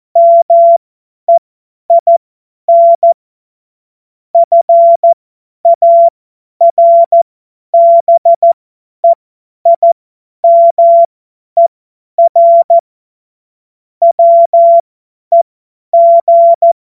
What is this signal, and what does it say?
Morse code, heard as a keyed tone, about 14 wpm, spaced as for 10 wpm, 680 Hz. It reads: MEIN FARBEIMER WEG
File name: Morsekurs / Witze / Witz 30. Dezember 2025 50 F70